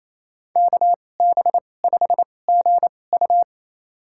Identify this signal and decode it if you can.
Morse code, heard as a keyed tone, about 28 wpm, spaced as for 20 wpm, 700 Hz. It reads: KB5GU
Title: USA Call Signs - 5 Characters - 2x Character Spacing - 28wpm